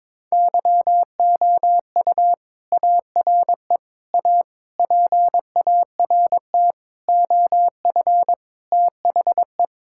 Morse code, heard as a keyed tone, about 22 wpm, 700 Hz. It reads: YOU ARE A PART OF THE
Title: Sentences from Top 1000 Words - Review Part 6 - 22wpm